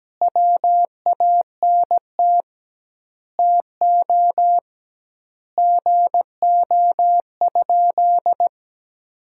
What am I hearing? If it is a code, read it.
Morse code, heard as a keyed tone, about 17 wpm, 700 Hz. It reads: WANT TO GO?
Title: 2x Word-Spacing - Sentences from Top 300 Words - 17wpm